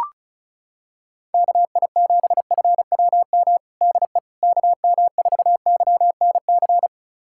Morse code, heard as a keyed tone, about 34 wpm, 700 Hz. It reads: KI7FWM DE KM4YNC
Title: USA Callsign DE Callsign - 6 Characters - 35wpm